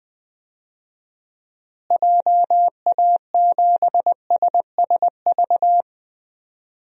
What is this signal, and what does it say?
Morse code, heard as a keyed tone, about 20 wpm, 700 Hz. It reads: JA7SSV
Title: DXCC-100 Call Signs - 6 Characters - Repeated 3x - 20wpm